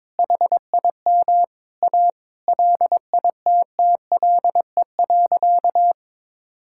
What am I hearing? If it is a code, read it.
Morse code, heard as a keyed tone, about 22 wpm, 700 Hz. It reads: HIM A LITTLE.
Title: Sentences from Top 1000 Words - Review Part 6 - 22wpm